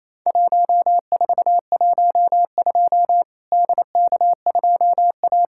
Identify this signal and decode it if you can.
Morse code, heard as a keyed tone, about 28 wpm, 700 Hz. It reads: 1412 DK2A
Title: Worked All Europe DX Contest - QTC Exchange - 28wpm